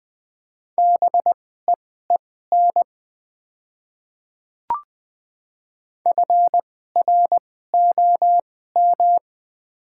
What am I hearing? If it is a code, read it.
Morse code, heard as a keyed tone, about 20 wpm, spaced as for 14 wpm, 700 Hz. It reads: BEEN FROM
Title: 2x Character Spacing - Top 200 Words - 20wpm